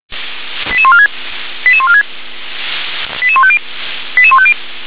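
Five-tone selective call (ZVEI), heard as quick runs of five tones, tones of about 70 ms. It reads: 88146 88146 88150 88150
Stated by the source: ZVEI five-tone call received on SDR.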